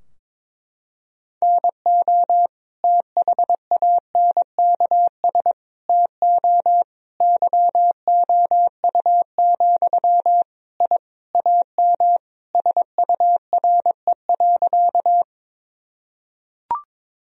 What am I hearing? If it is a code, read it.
Morse code, heard as a keyed tone, about 22 wpm, 700 Hz. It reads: NO THANKS TO YOU, I AM SURE.
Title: Sentences from Top 1000 Words - Review Part 3 - 22wpm